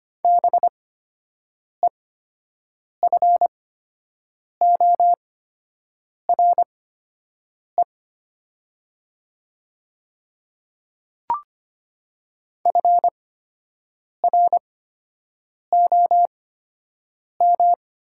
Morse code, heard as a keyed tone, about 25 wpm, spaced as for 7 wpm, 700 Hz. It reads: BEFORE FROM
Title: 8x Character Spacing - Top 200 Words - 25wpm